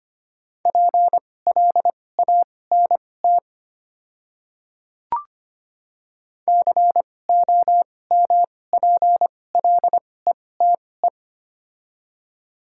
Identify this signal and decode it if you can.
Morse code, heard as a keyed tone, about 25 wpm, spaced as for 18 wpm, 700 Hz. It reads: PLANT COMPLETE
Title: 2x Character Spacing - Top 500 Words - 25wpm